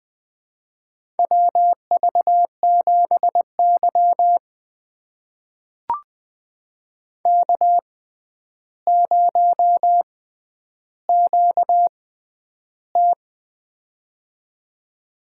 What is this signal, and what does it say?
Morse code, heard as a keyed tone, about 20 wpm, spaced as for 10 wpm, 700 Hz. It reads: WV7Y K0QT
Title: USA Call Signs - 4 Characters - 6x Character Spacing - 20wpm